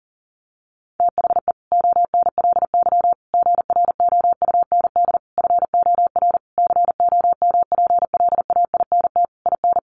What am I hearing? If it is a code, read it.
Morse code, heard as a keyed tone, about 40 wpm, 700 Hz. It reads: THE ONLY GROUND FOR COMPLAINT IN
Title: Sentences from Top 1400 Words 40wpm